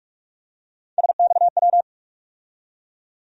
Morse code, heard as a keyed tone, about 45 wpm, 700 Hz. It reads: SXW
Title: USA Call Sign Postfixes - 3 Letters 45wpm